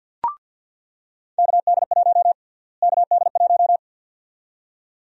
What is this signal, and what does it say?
Morse code, heard as a keyed tone, about 49 wpm, 700 Hz. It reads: KD1 KD1
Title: USA Call Sign Prefixes - Repeated 2x - 50wpm